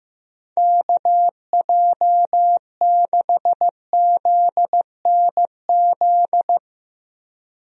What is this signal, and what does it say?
Morse code, heard as a keyed tone, about 15 wpm, 700 Hz. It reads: KJ6ZNZ
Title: USA Call Signs - 6 Characters - Repeated 3x - 15wpm